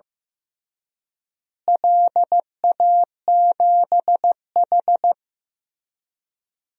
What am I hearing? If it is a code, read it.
Morse code, heard as a keyed tone, about 15 wpm, 700 Hz. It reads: LA7H
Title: DXCC-100 Call Signs - 4 Characters - Repeated 3x - 15wpm